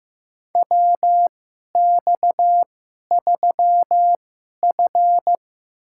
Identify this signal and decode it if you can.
Morse code, heard as a keyed tone, about 15 wpm, spaced as for 11 wpm, 700 Hz. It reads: WX3F
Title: USA Call Signs - 2x Character Spacing - 15wpm